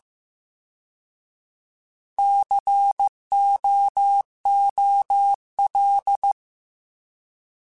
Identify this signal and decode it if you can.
Morse code, heard as a keyed tone, about 15 wpm, 800 Hz. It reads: COOL
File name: Alphabetical / Alphabetical mp3s / Cs @13